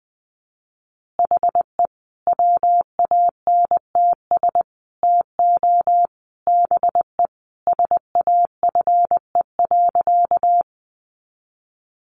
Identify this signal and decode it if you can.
Morse code, heard as a keyed tone, about 20 wpm, 700 Hz. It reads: HE WANTS TO BE SAFE.
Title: Additional Easy Phrases 20wpm